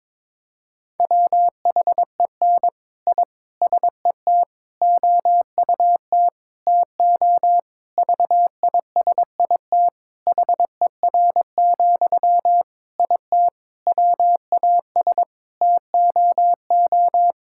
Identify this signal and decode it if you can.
Morse code, heard as a keyed tone, about 21 wpm, 700 Hz. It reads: WHEN I SET OUT TO VISIT HER, IT WAS TOO
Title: Sentences from Top 1000 Words - Review Part 6 - 22wpm